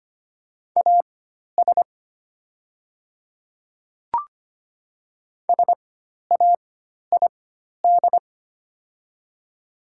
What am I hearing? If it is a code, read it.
Morse code, heard as a keyed tone, about 25 wpm, spaced as for 12 wpm, 700 Hz. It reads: AS SAID
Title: 4x Character Spacing - Top 300 Words - 25wpm